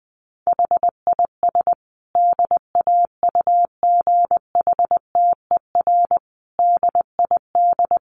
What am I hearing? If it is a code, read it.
Morse code, heard as a keyed tone, about 20 wpm, 700 Hz. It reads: HIS DAUGHTER DID
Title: Sentences from Top 1500 Words 20wpm